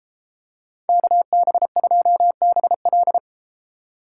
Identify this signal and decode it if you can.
Morse code, heard as a keyed tone, about 31 wpm, 700 Hz. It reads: KB2BL